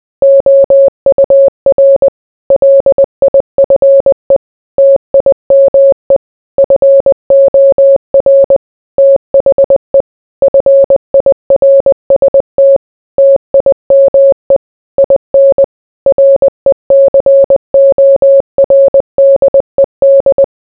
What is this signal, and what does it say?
Morse code, heard as a keyed tone, about 20 wpm, 550 Hz. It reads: OUR LIFE TIME FOR THE FIRST TIME IN RECORDED